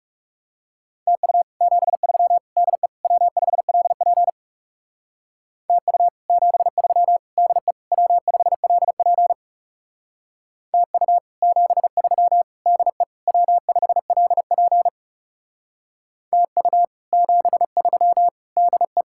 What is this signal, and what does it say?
Morse code, heard as a keyed tone, about 37 wpm, 700 Hz. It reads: TU 73 DE W5LP TU 73 DE W5LP TU 73 DE W5LP TU 73 DE
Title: Speed-Racing Common QSO Phrases Encore - 30wpm